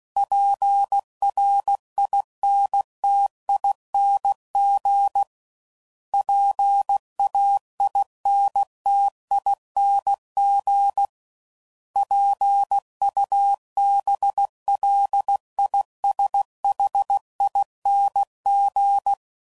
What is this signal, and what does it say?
Morse code, heard as a keyed tone, about 16 wpm, 800 Hz. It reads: PRINTING PAINTING PUBLISHING
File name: Suffixes / Special INGs / Sngl Ings